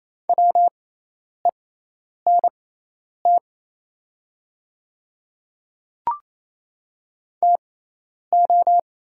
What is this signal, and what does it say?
Morse code, heard as a keyed tone, about 28 wpm, spaced as for 10 wpm, 700 Hz. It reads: WENT TO
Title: 6x Character Spacing - Top 200 Words - 28wpm